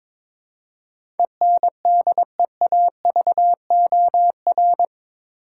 Morse code, heard as a keyed tone, about 21 wpm, 700 Hz. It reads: ENDEAVOR
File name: practice / 22wpm / Eight Letter Words - 22wpm